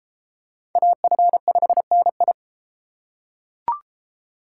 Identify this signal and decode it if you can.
Morse code, heard as a keyed tone, about 32 wpm, 700 Hz. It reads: AF5NI